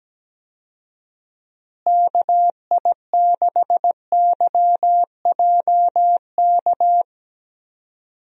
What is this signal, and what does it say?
Morse code, heard as a keyed tone, about 17 wpm, 700 Hz. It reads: KI6YJK
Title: USA Call Signs - 6 Characters - Repeated 3x - 17wpm